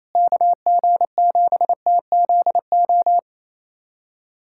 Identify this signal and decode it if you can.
Morse code, heard as a keyed tone, about 28 wpm, 700 Hz. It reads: KG7TZO
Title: USA Callsign DE Callsign - 6 Characters - 28wpm